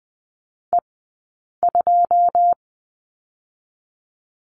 Morse code, heard as a keyed tone, about 20 wpm, spaced as for 8 wpm, 700 Hz. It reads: E2
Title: 080 Lesson 14d - Number 2 - Two Characters 20wpm